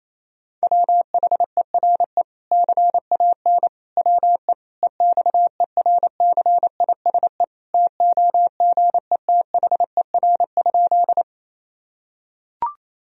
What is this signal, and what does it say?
Morse code, heard as a keyed tone, about 28 wpm, 700 Hz. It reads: WHERE CAN WE EXERCISE TOGETHER?
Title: Sentences from Top 1000 Words - Review Part 4 - 28wpm